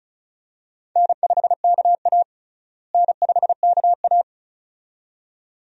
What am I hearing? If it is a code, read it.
Morse code, heard as a keyed tone, about 36 wpm, 700 Hz. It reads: N5KA N5KA
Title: USA Call Signs - 4 Characters - Repeated 2x - 35wpm